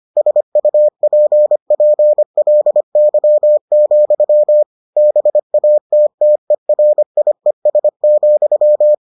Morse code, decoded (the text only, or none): SUPPLY, BATTERIES,